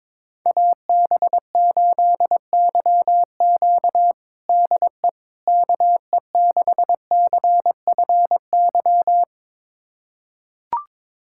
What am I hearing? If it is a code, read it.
Morse code, heard as a keyed tone, about 22 wpm, 700 Hz. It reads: AB8YQ DE KE6CFY
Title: USA Callsign DE Callsign 22wpm